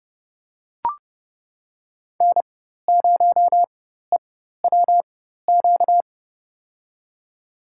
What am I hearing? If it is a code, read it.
Morse code, heard as a keyed tone, about 32 wpm, spaced as for 14 wpm, 700 Hz. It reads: N0EWQ
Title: USA Call Signs - 5 Characters - 4x Character Spacing - 30wpm